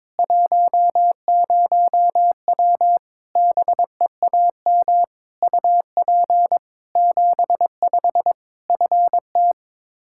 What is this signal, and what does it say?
Morse code, heard as a keyed tone, about 22 wpm, 700 Hz. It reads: 10W BEAM UP 75 FT
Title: Practice QSO - Exchange 2 - Info - 22wpm